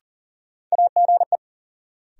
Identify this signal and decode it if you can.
Morse code, heard as a keyed tone, about 40 wpm, 700 Hz. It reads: AGE
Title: Sentences from Top 500 Words - New Word Drill - 40wpm